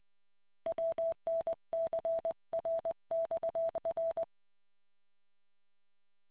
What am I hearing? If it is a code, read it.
Morse code, heard as a keyed tone, about 20 wpm, 660 Hz. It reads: WNCRXR